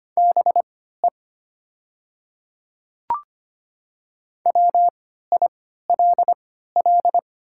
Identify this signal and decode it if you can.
Morse code, heard as a keyed tone, about 25 wpm, spaced as for 14 wpm, 700 Hz. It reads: BE WILL
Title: 3x Character Spacing - Top 100 Words - 25wpm